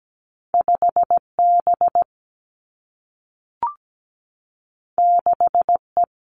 Morse code, heard as a keyed tone, about 17 wpm, 700 Hz. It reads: 5B 6E
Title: Two Challenging Letter-Number Combinations 17wpm